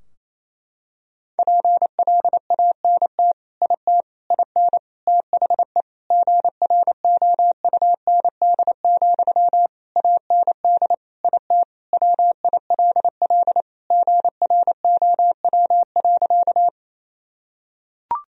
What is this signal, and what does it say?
Morse code, heard as a keyed tone, about 28 wpm, 700 Hz. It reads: PLANT IT IN THE GROUND, AND IT WILL GROW.
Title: Sentences from Top 1000 Words - Review Part 2 - 28wpm